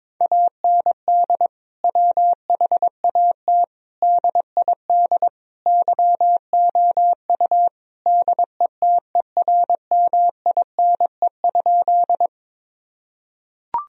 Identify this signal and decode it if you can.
Morse code, heard as a keyed tone, about 22 wpm, 700 Hz. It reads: AND WHAT DID YOU DETERMINE?
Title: Sentences from Top 1000 Words - Review Part 4 - 22wpm